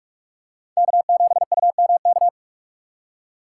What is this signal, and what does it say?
Morse code, heard as a keyed tone, about 42 wpm, 700 Hz. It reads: K7UMK